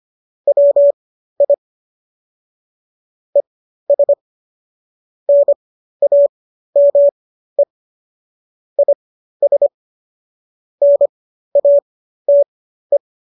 Morse code, heard as a keyed tone, about 25 wpm, spaced as for 13 wpm, 570 Hz. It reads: WI ES NAME IS NATE
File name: qso211 13